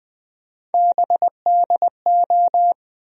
Morse code, heard as a keyed tone, about 20 wpm, 700 Hz. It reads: BDO
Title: USA Call Sign Postfixes - 3 Letters 20wpm